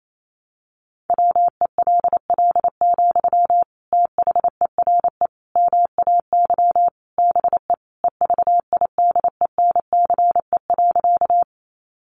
Morse code, heard as a keyed tone, about 28 wpm, 700 Hz. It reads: WELL, THERE MAY BE EVIDENCE.
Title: Sentences from Top 1300 Words 28wpm